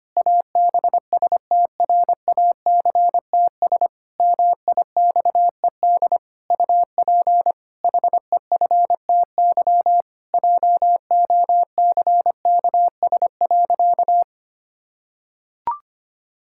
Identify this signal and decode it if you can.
Morse code, heard as a keyed tone, about 25 wpm, 700 Hz. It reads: ABSTRACTS MIXED UP HEFTY JOCKS.
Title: Pangram Challenge v2 - 25wpm